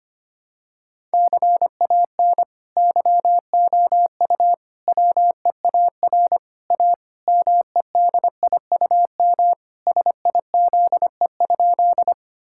Morse code, heard as a keyed tone, about 25 wpm, 700 Hz. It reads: CAN YOU WEAR A MEDIUM SIZE?
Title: Sentences from Top 1000 Words - Review Part 6 - 25wpm